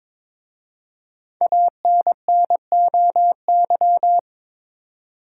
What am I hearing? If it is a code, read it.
Morse code, heard as a keyed tone, about 22 wpm, 700 Hz. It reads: ANNOY
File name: 5 Letter Words - First 3 Letters Given - 22wpm